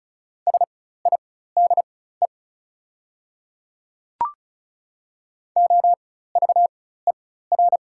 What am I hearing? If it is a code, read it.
Morse code, heard as a keyed tone, about 36 wpm, spaced as for 16 wpm, 700 Hz. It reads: SIDE OVER
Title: 4x Character Spacing - Top 200 Words - 35wpm